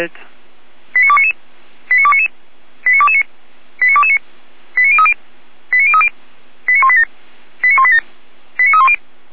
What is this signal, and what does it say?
Five-tone selective call (ZVEI), heard as quick runs of five tones, tones of about 70 ms, with speech. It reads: 89299 89299 89229 89229 89030 89030 89187 89187 89310